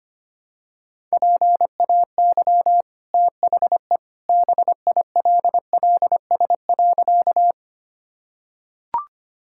Morse code, heard as a keyed tone, about 25 wpm, 700 Hz. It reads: PAY THE BILLS.
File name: Easy Sentences with Hints - 25wpm